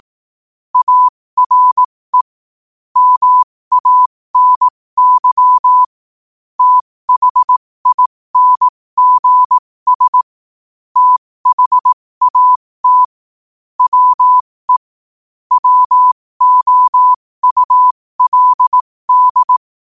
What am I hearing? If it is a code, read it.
Morse code, heard as a keyed tone, about 18 wpm, spaced as for 15 wpm, 1000 Hz. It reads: ARE MANY THINGS THAT WE WOULD
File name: qotdfiles / 15wpm